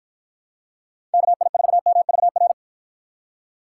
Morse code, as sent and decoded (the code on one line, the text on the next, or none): -.- .. ....- -- ...- .-.
KI4MVR